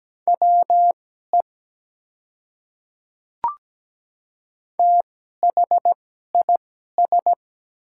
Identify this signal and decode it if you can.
Morse code, heard as a keyed tone, about 17 wpm, spaced as for 12 wpm, 700 Hz. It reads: WE THIS
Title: 2x Character Spacing - Top 100 Words - 17wpm